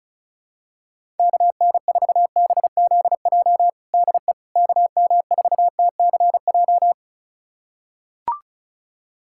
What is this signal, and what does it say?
Morse code, heard as a keyed tone, about 36 wpm, 700 Hz. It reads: KN4BZJ DE KM4TCJ
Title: USA Callsign DE Callsign - 6 Characters - 35wpm